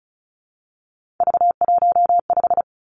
Morse code, heard as a keyed tone, about 36 wpm, 700 Hz. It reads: V15
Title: Three Challenging Letter-Number Combinations 35wpm